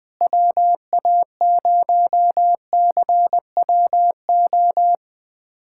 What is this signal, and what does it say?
Morse code, heard as a keyed tone, about 20 wpm, 700 Hz. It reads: WA0CWO
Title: USA Callsign DE Callsign - 6 Characters - 20wpm